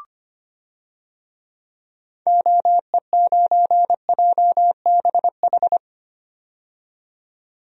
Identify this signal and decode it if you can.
Morse code, heard as a keyed tone, about 25 wpm, 700 Hz. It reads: OE9JBH
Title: DXCC-100 Call Signs - 6 Characters - 25wpm